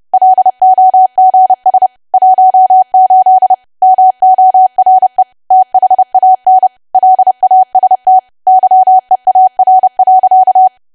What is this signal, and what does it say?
Morse code, heard as a keyed tone, about 30 wpm, 750 Hz. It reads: LOGS 18 MORE THAN LAST YEAR.
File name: cwa / W1AW-Level3 / 30WPM